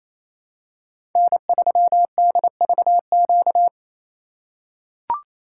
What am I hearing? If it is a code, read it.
Morse code, heard as a keyed tone, about 27 wpm, 700 Hz. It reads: N3DVQ
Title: USA Call Signs - 4x Character Spacing - 28wpm